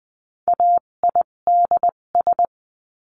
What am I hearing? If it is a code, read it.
Morse code, heard as a keyed tone, about 20 wpm, spaced as for 17 wpm, 700 Hz. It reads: AIDS
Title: 257 Lesson 42b - Words 14-20wpm